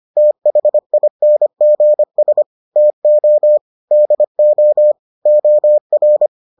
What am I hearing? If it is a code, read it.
Morse code, heard as a keyed tone, about 25 wpm, 600 Hz. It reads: THINGS TO DO OR